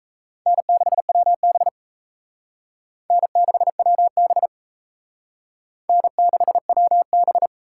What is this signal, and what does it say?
Morse code, heard as a keyed tone, about 35 wpm, 700 Hz. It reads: N6WB N6WB N6WB
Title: Speed Racing - USA 1x2 Call Signs 28wpm